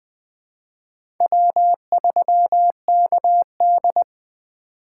Morse code, heard as a keyed tone, about 21 wpm, 700 Hz. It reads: W3KD
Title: USA Call Signs - 4 Characters - 4x Character Spacing - 20wpm